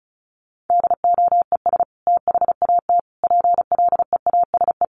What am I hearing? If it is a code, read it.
Morse code, heard as a keyed tone, about 34 wpm, 700 Hz. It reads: DOES THAT PLEASE
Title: Sentences from Top 1500 Words 35wpm